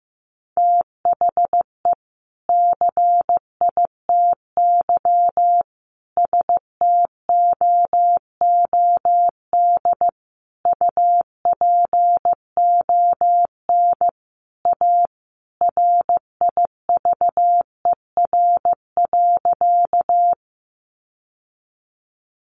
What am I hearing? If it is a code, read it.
Morse code, heard as a keyed tone, about 15 wpm, 700 Hz. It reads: THE CITY STOOD UPON A RIVER.
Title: Sentences from Top 1100 Words 15wpm